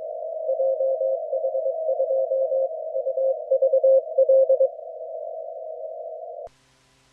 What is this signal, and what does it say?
Morse code, heard as a keyed tone, about 22 wpm, 530 Hz. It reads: JH2UVL